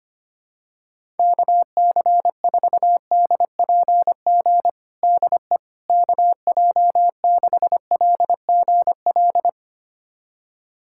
Morse code, heard as a keyed tone, about 25 wpm, 700 Hz. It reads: KC4DPG DE KJ6LGL
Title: USA Callsign DE Callsign - 6 Characters - 25wpm